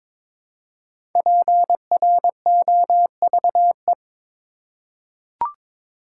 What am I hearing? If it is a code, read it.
Morse code, heard as a keyed tone, about 22 wpm, 700 Hz. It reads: PROVE
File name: 5 Letter Words - First 3 Letters Given - 22wpm